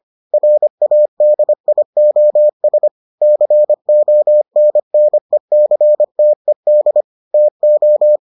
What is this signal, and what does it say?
Morse code, heard as a keyed tone, about 25 wpm, 600 Hz. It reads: RADIOS CONNECTED TO